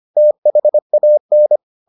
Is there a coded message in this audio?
Yes: Morse code from 0.2 to 1.6 s, heard as a keyed tone, about 25 wpm, 600 Hz. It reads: THAN